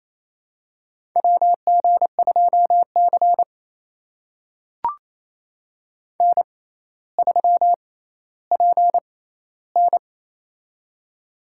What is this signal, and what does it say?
Morse code, heard as a keyed tone, about 28 wpm, spaced as for 14 wpm, 700 Hz. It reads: WG2C N3PN
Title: USA Call Signs - 4 Characters - 6x Character Spacing - 28wpm